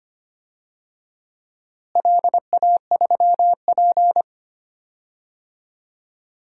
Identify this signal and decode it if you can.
Morse code, heard as a keyed tone, about 25 wpm, 700 Hz. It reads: LA3P
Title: DXCC-100 Call Signs - 4 Characters - Repeated 2x - 25wpm